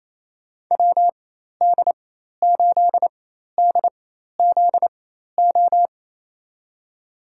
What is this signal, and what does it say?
Morse code, heard as a keyed tone, about 28 wpm, spaced as for 13 wpm, 700 Hz. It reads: WD8DZO
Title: USA Call Signs - 6 Characters - 4x Character Spacing - 28wpm